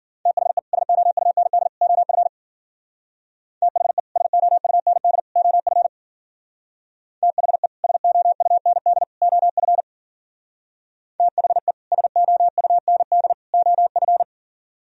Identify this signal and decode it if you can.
Morse code, heard as a keyed tone, about 48 wpm, 700 Hz. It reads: THE SOUND OF THE SOUND OF THE SOUND OF THE SOUND OF
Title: Speed-Racing - Sets of 3 Words 40wpm